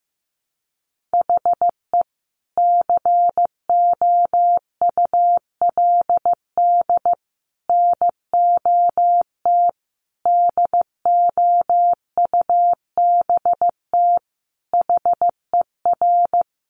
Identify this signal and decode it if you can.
Morse code, heard as a keyed tone, about 15 wpm, 700 Hz. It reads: HE COULD NOT DOUBT HER
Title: Sentences from Top 1500 Words 15wpm